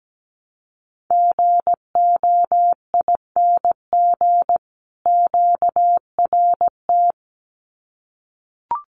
Morse code, heard as a keyed tone, about 17 wpm, 700 Hz. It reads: GOING QRT
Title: Common QSO Phrases 17wpm